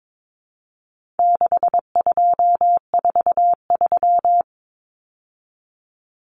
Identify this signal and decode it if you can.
Morse code, heard as a keyed tone, about 22 wpm, 700 Hz. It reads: 6243
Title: Numbers - 4-Digit 22wpm